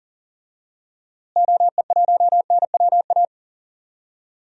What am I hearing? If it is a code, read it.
Morse code, heard as a keyed tone, about 40 wpm, 700 Hz. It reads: OE1NWA
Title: DXCC-100 Call Signs - 6 Characters - Repeated 2x - 40wpm